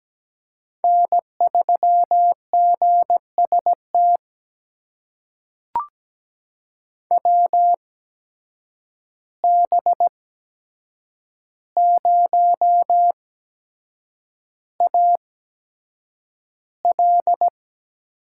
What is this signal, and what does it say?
Morse code, heard as a keyed tone, about 17 wpm, spaced as for 7 wpm, 700 Hz. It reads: N3GST WB0AL
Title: USA Call Signs - 5 Characters - 8x Character Spacing - 17wpm